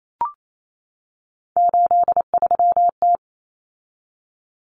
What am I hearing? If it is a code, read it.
Morse code, heard as a keyed tone, about 28 wpm, 700 Hz. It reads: 83T
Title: Common Cut Numbers - 3-Digit 28wpm